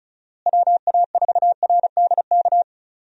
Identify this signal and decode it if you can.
Morse code, heard as a keyed tone, about 36 wpm, 700 Hz. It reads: WA4RDK
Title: USA Call Signs - 6 Characters - Repeated 3x - 35wpm